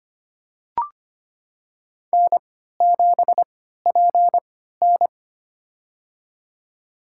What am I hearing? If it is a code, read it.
Morse code, heard as a keyed tone, about 25 wpm, spaced as for 14 wpm, 700 Hz. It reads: N7PN
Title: USA Call Signs - 4 Characters - 3x Character Spacing - 25wpm